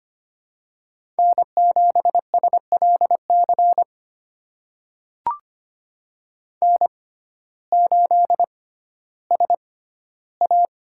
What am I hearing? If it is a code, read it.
Morse code, heard as a keyed tone, about 25 wpm, spaced as for 14 wpm, 700 Hz. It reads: N7SLC N8SA
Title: USA Call Signs - 5 Characters - 6x Character Spacing - 25wpm